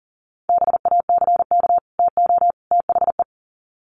Morse code, heard as a keyed tone, about 40 wpm, 700 Hz. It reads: BACK TO THE
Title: Sets of 3 Words 40wpm